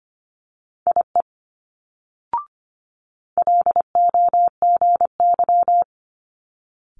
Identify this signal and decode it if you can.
Morse code, heard as a keyed tone, about 25 wpm, 700 Hz. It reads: IE LOGY
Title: Multi-letter Phonemes Quick Fire 25wpm